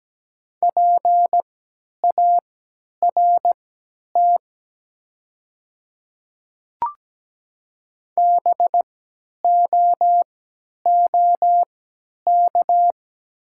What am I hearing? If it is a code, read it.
Morse code, heard as a keyed tone, about 17 wpm, spaced as for 10 wpm, 700 Hz. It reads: PART BOOK
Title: 3x Character Spacing - Top 300 Words - 17wpm